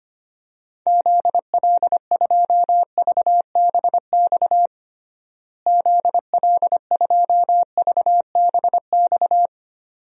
Morse code, heard as a keyed tone, about 25 wpm, 700 Hz. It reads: ZL2VBX ZL2VBX
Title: DXCC-100 Call Signs - 6 Characters - Repeated 2x - 25wpm